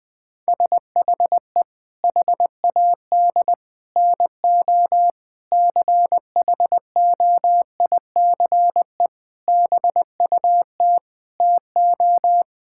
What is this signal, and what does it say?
Morse code, heard as a keyed tone, about 20 wpm, 700 Hz. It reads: SHE HAD NO CHOICE BUT TO
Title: Sentences with 5-Word Phrases - Spoken Beginning - 20wpm